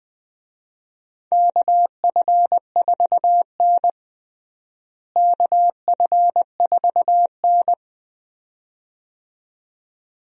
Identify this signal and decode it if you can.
Morse code, heard as a keyed tone, about 20 wpm, 700 Hz. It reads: KF4N KF4N
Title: USA Call Signs - 4 Characters - Repeated 2x - 20wpm